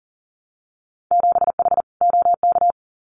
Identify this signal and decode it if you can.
Morse code, heard as a keyed tone, about 38 wpm, 700 Hz. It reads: 7H OK
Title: Winter Field Day Exchanges Encore 40wpm